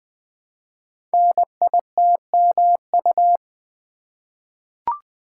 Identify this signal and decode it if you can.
Morse code, heard as a keyed tone, about 20 wpm, 700 Hz. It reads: NITMU
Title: Code Groups - 5 Letter:Numbers - 20wpm